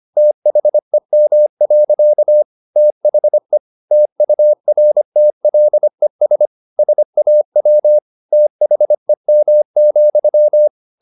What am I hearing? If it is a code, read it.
Morse code, heard as a keyed tone, about 25 wpm, 600 Hz. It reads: THEM. THE TURTLES SAW THEM,
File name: the old man and the sea part 1 25wpm